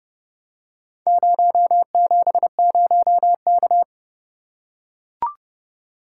Morse code, heard as a keyed tone, about 29 wpm, 700 Hz. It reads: 070K